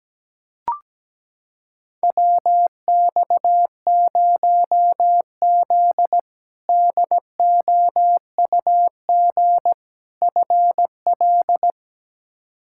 Morse code, heard as a keyed tone, about 17 wpm, 700 Hz. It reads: WX0Z DOUG FL
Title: K1USN SST QSO Practice - 17wpm